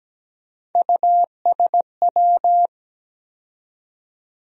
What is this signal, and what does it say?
Morse code, heard as a keyed tone, about 17 wpm, 700 Hz. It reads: USW
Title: USA Call Sign Postfixes - 3 Letters 17wpm